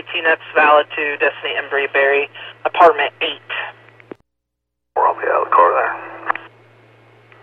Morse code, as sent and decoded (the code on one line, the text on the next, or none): none